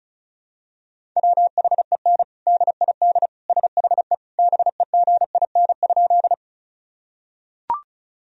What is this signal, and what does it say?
Morse code, heard as a keyed tone, about 36 wpm, 700 Hz. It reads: WHEN DID SHE BEGIN?
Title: Sentences from Top 1000 Words - Review Part 1 - 35wpm